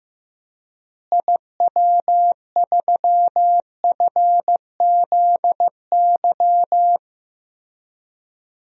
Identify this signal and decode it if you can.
Morse code, heard as a keyed tone, about 16 wpm, 700 Hz. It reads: IW3FZY
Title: DXCC-100 Call Signs - 6 Characters - Repeated 3x - 15wpm